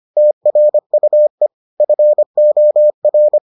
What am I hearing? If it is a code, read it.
Morse code, heard as a keyed tone, about 25 wpm, 600 Hz. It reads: TRUE FOR